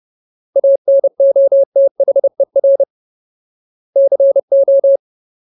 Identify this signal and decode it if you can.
Morse code, heard as a keyed tone, about 30 wpm, 550 Hz. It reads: ANOTHER CO
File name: Ultimate Word List / 30WPM Ultimate